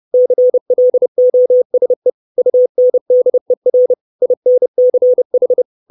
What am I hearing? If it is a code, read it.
Morse code, heard as a keyed tone, about 30 wpm, 500 Hz. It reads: CLOSE UNDER INCH
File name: 632 Words 30WPM